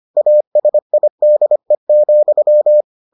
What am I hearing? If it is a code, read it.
Morse code, heard as a keyed tone, about 25 wpm, 600 Hz. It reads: ASIDE,